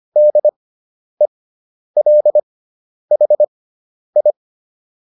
Morse code, heard as a keyed tone, about 25 wpm, spaced as for 10 wpm, 600 Hz. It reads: DELHI